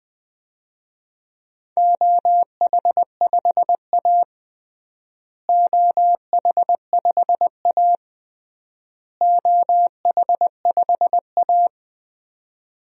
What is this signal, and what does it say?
Morse code, heard as a keyed tone, about 21 wpm, 700 Hz. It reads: OH5A OH5A OH5A
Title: DXCC-100 Call Signs - 4 Characters - Repeated 3x - 20wpm